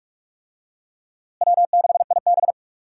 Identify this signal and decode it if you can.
Morse code, heard as a keyed tone, about 46 wpm, 700 Hz. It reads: W6IB
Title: USA Call Signs - 4 Characters - Repeated 3x - 45wpm